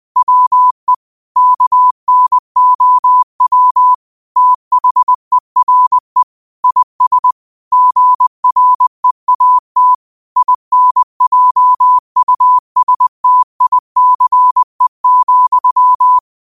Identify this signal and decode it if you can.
Morse code, heard as a keyed tone, about 20 wpm, 1000 Hz. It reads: WE KNOW THERE IS GREAT INJUSTICE,